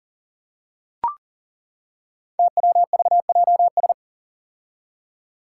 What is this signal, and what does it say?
Morse code, heard as a keyed tone, about 38 wpm, 700 Hz. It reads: TWVJS
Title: Five Challenging Letter-Number Combinations 40wpm